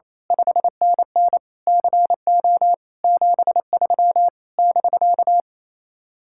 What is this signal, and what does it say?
Morse code, heard as a keyed tone, about 26 wpm, 700 Hz. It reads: 5NN CO 73 <BK>
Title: Speed-Racing Common QSO Phrases Encore - 28wpm